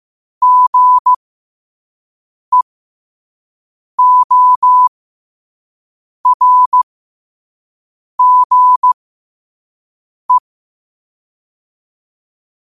Morse code, heard as a keyed tone, about 15 wpm, spaced as for 5 wpm, 1000 Hz. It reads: GEORGE
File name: qotdfiles / 5wpm